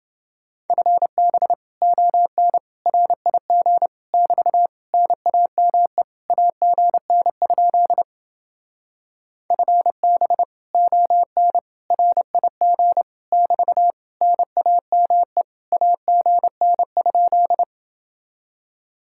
Morse code, heard as a keyed tone, about 29 wpm, 700 Hz. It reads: FB ON RIG <BT> NAME AGN? FB ON RIG <BT> NAME AGN?